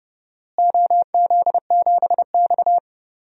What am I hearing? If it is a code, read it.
Morse code, heard as a keyed tone, about 30 wpm, 700 Hz. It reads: OZ7X
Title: DXCC-100 Call Signs - 4 Characters - Repeated 3x - 30wpm